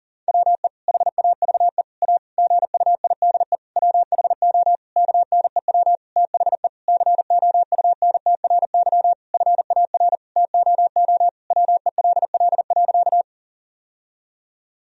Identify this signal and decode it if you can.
Morse code, heard as a keyed tone, about 40 wpm, 700 Hz. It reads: WE HAVE A GUIDE WHO KNEW THE COUNTRY FAR TOO WELL.
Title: Sentences from Top 1000 Words - Review Part 3 - 40wpm